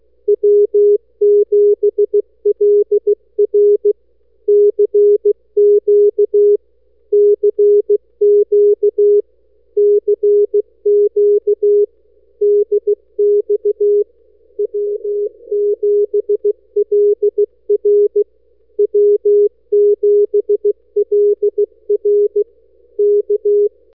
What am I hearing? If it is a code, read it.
Morse code, heard as a keyed tone, about 16 wpm, 410 Hz. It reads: W7LR CQ CQ CQ DX W7LR W7LR K
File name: n6tr / w7lr cq 15-Mar-2009